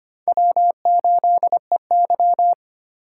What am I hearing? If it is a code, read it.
Morse code, heard as a keyed tone, about 25 wpm, 700 Hz. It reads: W8EY
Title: USA Call Signs - 4x Character Spacing - 25wpm